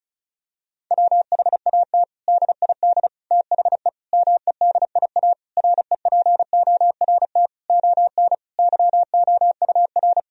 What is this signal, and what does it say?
Morse code, heard as a keyed tone, about 36 wpm, 700 Hz. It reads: WHAT DID THE MEDIA REPORT ON YOUR
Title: Sentences from Top 1000 Words - Review Part 4 - 35wpm